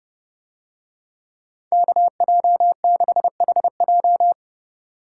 Morse code, heard as a keyed tone, about 29 wpm, 700 Hz. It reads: KJ6HJ